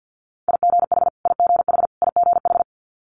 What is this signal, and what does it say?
Morse code, heard as a keyed tone, about 50 wpm, 700 Hz. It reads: IGH IGH IGH
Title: Multi-letter Phonemes Repeated 3x 50wpm